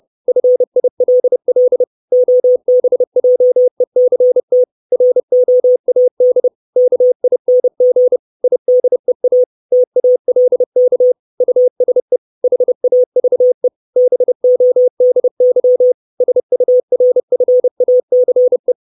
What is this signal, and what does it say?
Morse code, heard as a keyed tone, about 30 wpm, 500 Hz. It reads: FILL OBJECT ROAD KING IDEA TALK USE HAVE BODY SURFACE
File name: 632 Words 30WPM